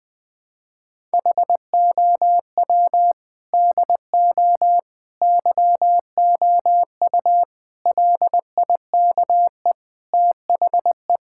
Morse code, heard as a keyed tone, about 20 wpm, 700 Hz. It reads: HOW DO YOU LIKE THE
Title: Sentences from Top 1000 Words - Review Part 6 - 20wpm